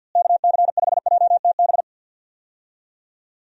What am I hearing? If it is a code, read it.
Morse code, heard as a keyed tone, about 51 wpm, 700 Hz. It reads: KC5JTB
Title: USA Call Signs - 6 Characters - Repeated 3x - 50wpm